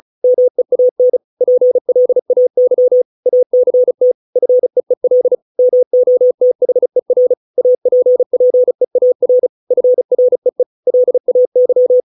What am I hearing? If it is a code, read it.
Morse code, heard as a keyed tone, about 34 wpm, 500 Hz. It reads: MEAN PLAY ACT FEEL MOTHER APPEAR FREE LAY